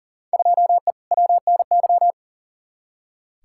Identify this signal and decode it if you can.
Morse code, heard as a keyed tone, about 40 wpm, 700 Hz. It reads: 2E WNY
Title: Field Day Exchanges Encore - 40wpm